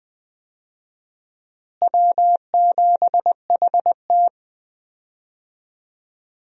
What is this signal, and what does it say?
Morse code, heard as a keyed tone, about 20 wpm, 700 Hz. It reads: W7HT
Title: USA Call Signs - 4 Characters - Repeated 3x - 20wpm